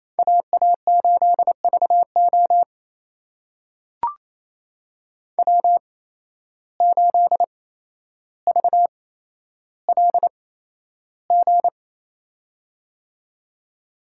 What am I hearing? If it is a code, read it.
Morse code, heard as a keyed tone, about 28 wpm, spaced as for 12 wpm, 700 Hz. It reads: AA8VO W8VLG
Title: USA Call Signs - 5 Characters - 8x Character Spacing - 28wpm